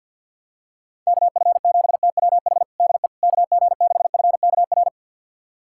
Morse code, heard as a keyed tone, about 50 wpm, 700 Hz. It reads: KF7TWH DE KG6FKR
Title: USA Callsign DE Callsign - 6 Characters - 50wpm